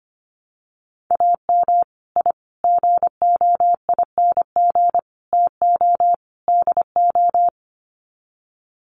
Morse code, heard as a keyed tone, about 25 wpm, 700 Hz. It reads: AM I GOING TO DO
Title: Sets of 5 Words Encore 25wpm